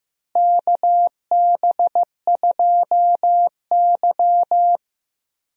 Morse code, heard as a keyed tone, about 15 wpm, 700 Hz. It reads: KB2Y
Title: USA Call Signs - 4 Characters - Repeated 3x - 15wpm